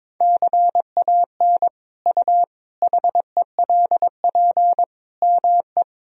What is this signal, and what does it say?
Morse code, heard as a keyed tone, about 22 wpm, 700 Hz. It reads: CAN U HELP ME
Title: Sentences from Top 200 Words and Abbreviations 22wpm